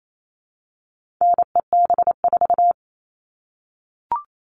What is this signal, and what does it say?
Morse code, heard as a keyed tone, about 28 wpm, 700 Hz. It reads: NEB4
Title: All Cut Numbers - 4-Digit 28wpm